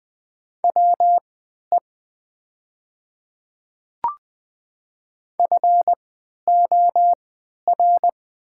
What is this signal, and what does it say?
Morse code, heard as a keyed tone, about 20 wpm, spaced as for 11 wpm, 700 Hz. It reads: WE FOR
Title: 3x Character Spacing - Top 200 Words - 20wpm